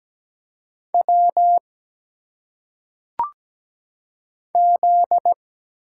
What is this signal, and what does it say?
Morse code, heard as a keyed tone, about 17 wpm, 700 Hz. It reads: W Z
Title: Single Letters - Phonetic Pronunciation 17wpm